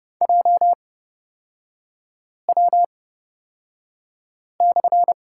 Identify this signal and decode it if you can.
Morse code, heard as a keyed tone, about 30 wpm, spaced as for 5 wpm, 690 Hz. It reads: JW/
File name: Lesson 37e - Letter x - Three Characters 30wpm